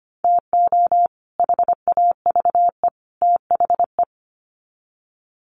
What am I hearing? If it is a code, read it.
Morse code, heard as a keyed tone, about 25 wpm, 700 Hz. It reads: TO HAVE THE
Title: Sets of 3 Words 25wpm